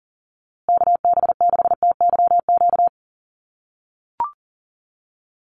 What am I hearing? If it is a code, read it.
Morse code, heard as a keyed tone, about 39 wpm, 700 Hz. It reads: KB6TYQ
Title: USA Call Signs Repeated 2x 40wpm